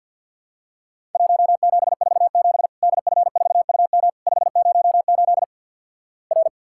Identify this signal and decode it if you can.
Morse code, heard as a keyed tone, about 50 wpm, 700 Hz. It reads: 1747 DF4UM 508 R
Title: Worked All Europe DX Contest - QTC Exchange with Repeats - 50wpm